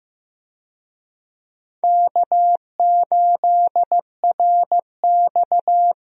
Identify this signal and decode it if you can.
Morse code, heard as a keyed tone, about 15 wpm, 700 Hz. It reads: K8RX
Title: USA Call Signs - 4 Characters - Repeated 2x - 15wpm